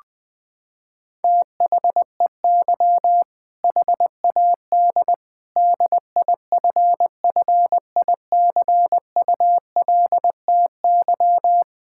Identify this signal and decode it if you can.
Morse code, heard as a keyed tone, about 20 wpm, 700 Hz. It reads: THEY HAD DIFFICULTY
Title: Sentences from Top 1600 Words 20wpm